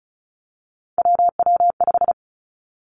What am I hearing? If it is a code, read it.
Morse code, heard as a keyed tone, about 36 wpm, 700 Hz. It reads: WW5
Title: Three Challenging Letter-Number Combinations 35wpm